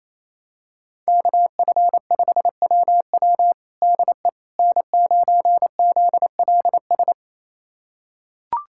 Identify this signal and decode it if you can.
Morse code, heard as a keyed tone, about 28 wpm, 700 Hz. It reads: KF5WW DE N9ZLS
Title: USA Callsign DE Callsign - 5 Characters - 28wpm